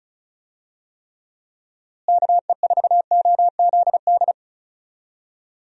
Morse code, heard as a keyed tone, about 36 wpm, 700 Hz. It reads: KE4OZD
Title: USA Call Signs - 6 Characters - Repeated 3x - 35wpm